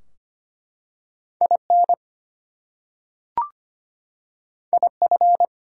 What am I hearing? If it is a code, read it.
Morse code, heard as a keyed tone, about 25 wpm, 700 Hz. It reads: IN IF
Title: Two Letter Words - 25wpm